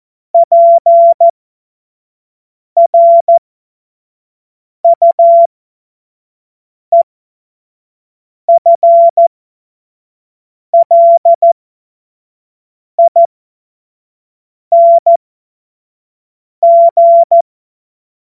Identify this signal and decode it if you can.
Morse code, heard as a keyed tone, about 14 wpm, spaced as for 5 wpm, 680 Hz. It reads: PRUEFLING
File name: Morsekurs / Witze / Witz 17. Mai 2022 25 F70